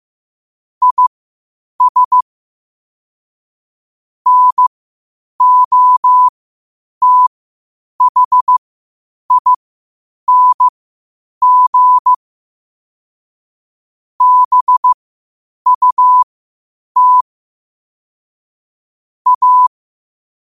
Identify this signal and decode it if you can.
Morse code, heard as a keyed tone, about 15 wpm, spaced as for 8 wpm, 1000 Hz. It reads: IS NOTHING BUT A
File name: 8wpm